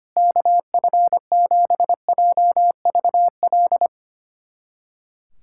Morse code, heard as a keyed tone, about 25 wpm, 700 Hz. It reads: KF7JVL
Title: USA Callsign DE Callsign - 6 Characters - 25wpm